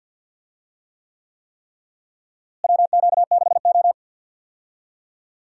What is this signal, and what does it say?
Morse code, heard as a keyed tone, about 47 wpm, 700 Hz. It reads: WQ6O
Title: USA Call Signs - 4 Characters - Repeated 3x - 50wpm